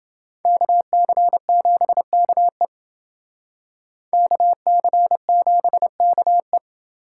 Morse code, heard as a keyed tone, about 30 wpm, 700 Hz. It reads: KC7KE KC7KE